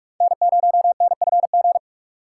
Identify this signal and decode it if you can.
Morse code, heard as a keyed tone, about 41 wpm, 700 Hz. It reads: N0NFG
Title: USA Call Signs - 5 Characters - 3x Character Spacing - 45wpm